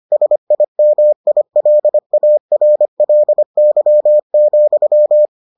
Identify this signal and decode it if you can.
Morse code, heard as a keyed tone, about 25 wpm, 600 Hz. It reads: SIMILARLY,